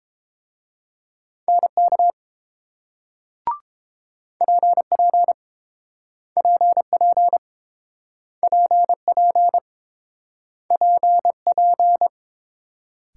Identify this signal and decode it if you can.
Morse code, heard as a keyed tone, about 31 wpm, 700 Hz. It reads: NK PP PP PP PP